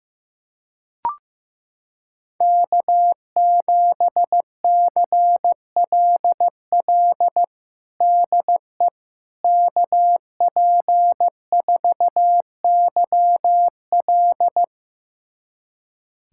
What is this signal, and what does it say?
Morse code, heard as a keyed tone, about 15 wpm, 700 Hz. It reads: K7CLL DE KP4YL
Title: USA Callsign DE Callsign - 5 Characters - 15wpm